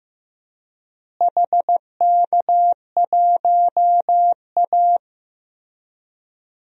Morse code, heard as a keyed tone, about 15 wpm, 700 Hz. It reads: HK1A
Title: DXCC-100 Call Signs - 4 Characters - Repeated 3x - 15wpm